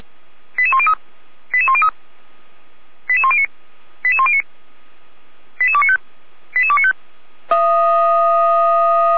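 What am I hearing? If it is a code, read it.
Five-tone selective call (ZVEI), heard as quick runs of five tones, tones of about 70 ms. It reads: 88103 88103 88109 88109 88295 88295
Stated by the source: Five-tone ZVEI selective call received by SDR.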